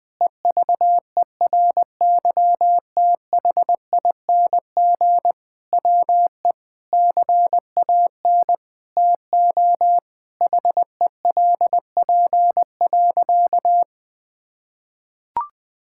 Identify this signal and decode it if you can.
Morse code, heard as a keyed tone, about 20 wpm, 700 Hz. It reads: EVERYTHING WE CAN TO HELP.
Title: Sentences with 6-Word Phrases - Spoken Beginning - 20wpm